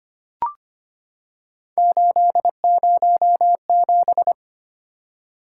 Morse code, heard as a keyed tone, about 25 wpm, 700 Hz. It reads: 807
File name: Three Letter-Number Combinations - 25wpm